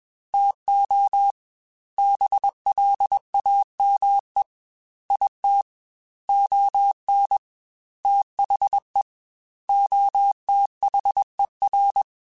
Morse code, heard as a keyed tone, about 21 wpm, 800 Hz. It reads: TO BLAME IT ON THE OTHER